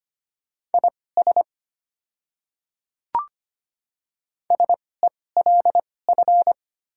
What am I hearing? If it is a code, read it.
Morse code, heard as a keyed tone, about 25 wpm, spaced as for 18 wpm, 700 Hz. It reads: IS SELF
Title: 2x Character Spacing - Top 300 Words - 25wpm